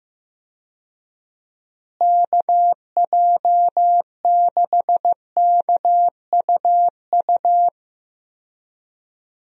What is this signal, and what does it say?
Morse code, heard as a keyed tone, about 14 wpm, 700 Hz. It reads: KJ6KUU
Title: USA Call Signs - 6 Characters - Repeated 3x - 15wpm